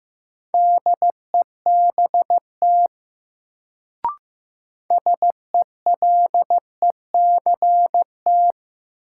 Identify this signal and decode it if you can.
Morse code, heard as a keyed tone, about 15 wpm, 700 Hz. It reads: DEBT SELECT
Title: Sentences from Top 1300 Words - New Word Drill - 15wpm